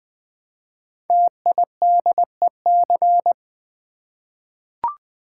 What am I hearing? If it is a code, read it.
Morse code, heard as a keyed tone, about 20 wpm, 700 Hz. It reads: TIDEC
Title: Code Groups - 5 Letters - 20wpm